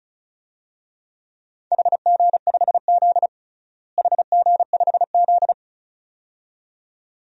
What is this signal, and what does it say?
Morse code, heard as a keyed tone, about 36 wpm, 700 Hz. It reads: HG5Z HG5Z
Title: DXCC-100 Call Signs - 4 Characters - Repeated 2x - 35wpm